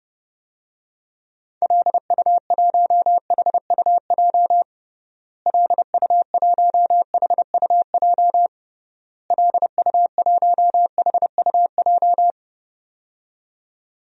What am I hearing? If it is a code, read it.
Morse code, heard as a keyed tone, about 32 wpm, 700 Hz. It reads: LU1HUJ LU1HUJ LU1HUJ
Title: DXCC-100 Call Signs - 6 Characters - Repeated 3x - 30wpm